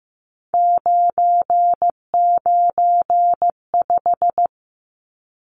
Morse code, heard as a keyed tone, about 15 wpm, 700 Hz. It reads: 995
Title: Numbers - 3-Digit 15wpm